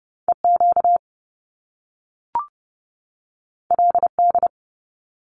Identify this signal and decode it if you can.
Morse code, heard as a keyed tone, about 30 wpm, 700 Hz. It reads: EQ LD
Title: Top Two Letter Combinations in Words 30wpm